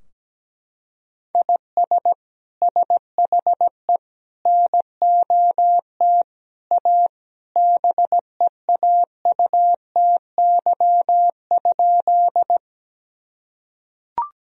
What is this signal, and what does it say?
Morse code, heard as a keyed tone, about 17 wpm, 700 Hz. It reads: IS SHE NOT A BEAUTY?
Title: Sentences from Top 1000 Words - Review Part 2 - 17wpm